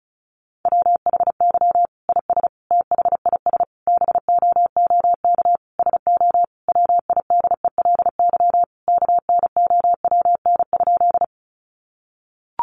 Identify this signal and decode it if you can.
Morse code, heard as a keyed tone, about 34 wpm, 700 Hz. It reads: WHY IS THIS BOOK SO WIDELY KNOWN?
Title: Sentences from Top 1200 Words 35wpm